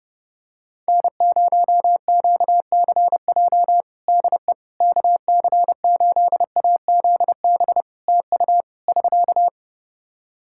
Morse code, heard as a keyed tone, about 30 wpm, 700 Hz. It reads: N0QCJ DE KC8AZB TU <SK>
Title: Practice QSO - Exchange 4 - Ending - 30wpm